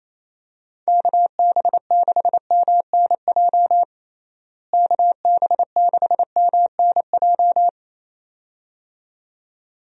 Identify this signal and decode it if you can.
Morse code, heard as a keyed tone, about 27 wpm, 700 Hz. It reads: KB6MNJ KB6MNJ
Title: USA Call Signs - 6 Characters - Repeated 2x - 28wpm